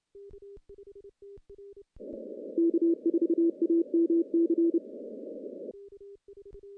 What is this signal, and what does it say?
Morse code, heard as a keyed tone, about 30 wpm, 340 Hz. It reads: K4AMC
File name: contests / CW-SS-k5tr- -1400